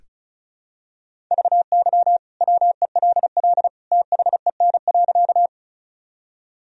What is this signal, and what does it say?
Morse code, heard as a keyed tone, about 36 wpm, 700 Hz. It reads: VY WELL THEN.
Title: Sentences from Top 200 Words and Abbreviations 35wpm